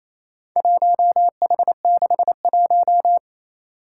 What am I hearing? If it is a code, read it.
Morse code, heard as a keyed tone, about 27 wpm, 700 Hz. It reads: 1H61
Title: Four Challenging Letter-Number Combinations 28wpm